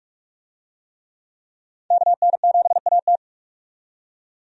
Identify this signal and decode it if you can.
Morse code, heard as a keyed tone, about 42 wpm, 700 Hz. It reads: KN7AT
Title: USA Call Signs - 5 Characters - 45wpm